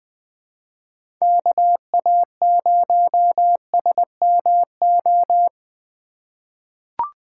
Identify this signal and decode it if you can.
Morse code, heard as a keyed tone, about 20 wpm, 700 Hz. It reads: KA0SMO
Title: USA Call Signs - 6 Characters - 3x Character Spacing - 20wpm